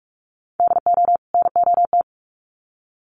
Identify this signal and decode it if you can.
Morse code, heard as a keyed tone, about 44 wpm, 700 Hz. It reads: DO NOT
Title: Sets of 2 Words 45wpm